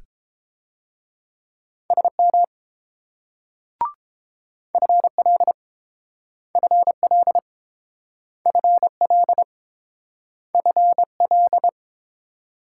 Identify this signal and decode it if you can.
Morse code, heard as a keyed tone, about 30 wpm, 700 Hz. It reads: SM FL FL FL FL